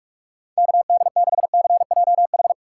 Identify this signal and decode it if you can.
Morse code, heard as a keyed tone, about 46 wpm, 700 Hz. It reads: KD6CJH
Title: USA Call Signs - 2x Character Spacing - 45wpm